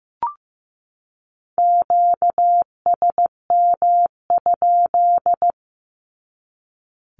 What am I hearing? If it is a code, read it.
Morse code, heard as a keyed tone, about 15 wpm, 700 Hz. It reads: QSM?
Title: Q-Codes 15wpm